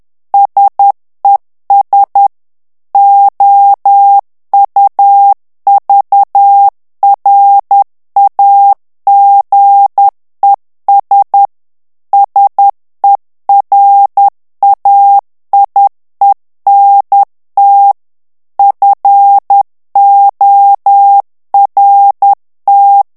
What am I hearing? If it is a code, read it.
Morse code, heard as a keyed tone, about 10 wpm, 800 Hz. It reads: SES OUVRAGES SERAIENT FORT
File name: morse mp3 / FRANCAIS TEXTE CLAIR 2 VITESSE 10 MOTS MINUTES